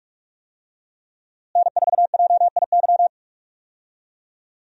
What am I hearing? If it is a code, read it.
Morse code, heard as a keyed tone, about 46 wpm, 700 Hz. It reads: N4JIY
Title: USA Call Signs - 5 Characters - Repeated 3x - 45wpm